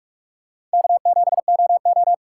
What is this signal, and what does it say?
Morse code, heard as a keyed tone, about 44 wpm, 700 Hz. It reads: K7OO